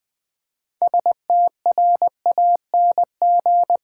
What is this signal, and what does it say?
Morse code, heard as a keyed tone, about 20 wpm, 700 Hz. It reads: STRANG
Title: Build A Word - 3 Letters At A Time - 8-Letter Words - 20wpm